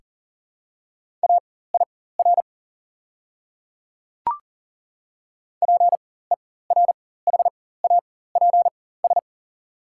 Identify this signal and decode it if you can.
Morse code, heard as a keyed tone, about 40 wpm, spaced as for 19 wpm, 700 Hz. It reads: AIR PERHAPS
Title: 4x Character Spacing - Top 500 Words - 40wpm